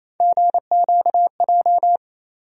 Morse code, heard as a keyed tone, about 28 wpm, 700 Hz. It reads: GQJ
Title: USA Call Sign Postfixes - 3 Letters 28wpm